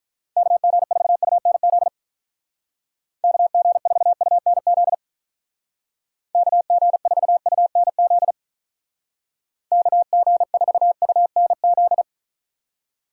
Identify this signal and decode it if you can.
Morse code, heard as a keyed tone, about 44 wpm, 700 Hz. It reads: KG4UNZ KG4UNZ KG4UNZ KG4UNZ